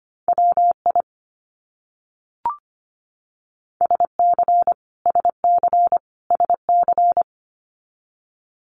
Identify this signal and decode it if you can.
Morse code, heard as a keyed tone, about 25 wpm, 700 Hz. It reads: WI SC SC SC
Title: US States Repeated 3x 25wpm